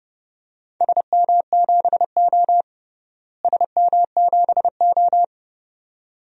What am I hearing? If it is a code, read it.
Morse code, heard as a keyed tone, about 30 wpm, 700 Hz. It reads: SM7O SM7O
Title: DXCC-100 Call Signs - 4 Characters - Repeated 2x - 30wpm